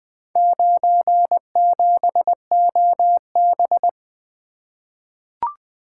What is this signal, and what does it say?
Morse code, heard as a keyed tone, about 20 wpm, 700 Hz. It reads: 97OB
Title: Four Letter-Number Combinations - 20wpm